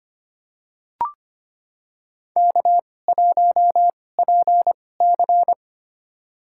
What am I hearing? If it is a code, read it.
Morse code, heard as a keyed tone, about 25 wpm, spaced as for 18 wpm, 700 Hz. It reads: K1PC
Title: USA Call Signs - 4 Characters - 2x Character Spacing - 25wpm